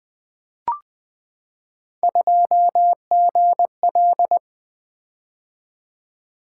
Morse code, heard as a keyed tone, about 20 wpm, 700 Hz. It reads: 2GL
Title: Three Letter-Number Combinations - 20wpm